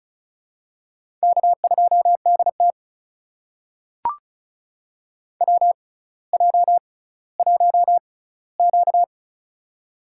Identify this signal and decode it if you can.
Morse code, heard as a keyed tone, about 36 wpm, spaced as for 18 wpm, 700 Hz. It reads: K2DT WJ1Q
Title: USA Call Signs - 4 Characters - 6x Character Spacing - 35wpm